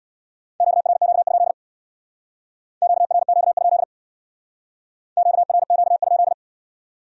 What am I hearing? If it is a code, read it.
Morse code, heard as a keyed tone, about 61 wpm, 700 Hz. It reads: QRQ? QRQ? QRQ?
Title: Speed-Racing Q-Codes - 50wpm